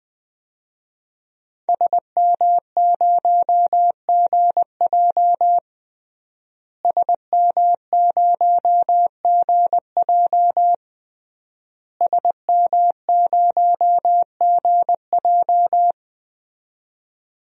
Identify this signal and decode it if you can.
Morse code, heard as a keyed tone, about 20 wpm, 700 Hz. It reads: SM0GJ SM0GJ SM0GJ
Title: DXCC-100 Call Signs - 5 Characters - Repeated 3x - 20wpm